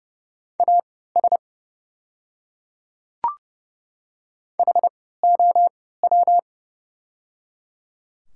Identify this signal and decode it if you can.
Morse code, heard as a keyed tone, about 30 wpm, spaced as for 17 wpm, 690 Hz. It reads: AS HOW
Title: 3x Character Spacing - Top 100 Words - 30wpm